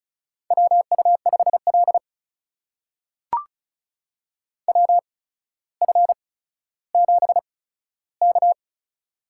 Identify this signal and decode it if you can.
Morse code, heard as a keyed tone, about 36 wpm, spaced as for 15 wpm, 700 Hz. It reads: WU5L WF7K
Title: USA Call Signs - 4 Characters - 8x Character Spacing - 35wpm